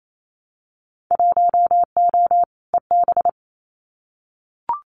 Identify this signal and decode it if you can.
Morse code, heard as a keyed tone, about 28 wpm, 700 Hz. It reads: 1O EB
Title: Winter Field Day Exchanges 28wpm